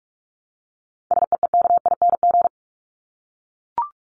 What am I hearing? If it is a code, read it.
Morse code, heard as a keyed tone, about 44 wpm, 700 Hz. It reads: SEEKING
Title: ING Suffix - Top 100 Words with Base Words 45wpm